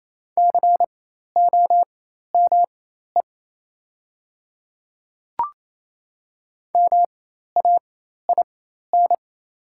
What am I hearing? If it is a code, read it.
Morse code, heard as a keyed tone, about 28 wpm, spaced as for 13 wpm, 700 Hz. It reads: COME MAIN
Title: 4x Character Spacing - Top 300 Words - 28wpm